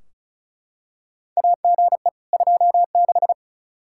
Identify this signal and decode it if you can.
Morse code, heard as a keyed tone, about 34 wpm, 700 Hz. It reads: AGE 26
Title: Practice QSO - Exchange 3 - More Info - 35wpm